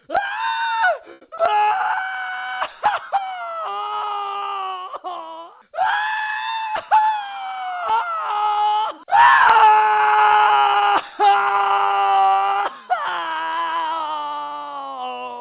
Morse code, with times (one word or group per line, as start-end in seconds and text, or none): none